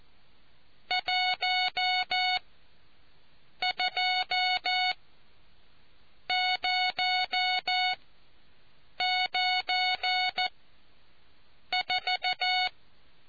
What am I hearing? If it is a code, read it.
Morse code, heard as a keyed tone, about 14 wpm, spaced as for 6 wpm, 740 Hz. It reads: 12094